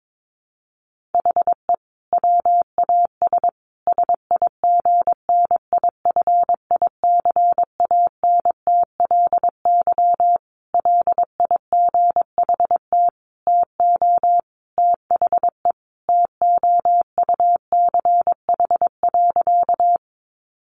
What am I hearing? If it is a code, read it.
Morse code, heard as a keyed tone, about 22 wpm, 700 Hz. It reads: HE WAS SIGNIFICANTLY LIGHT TO THE TOUCH.
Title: Sentences from Top 1500 Words 22wpm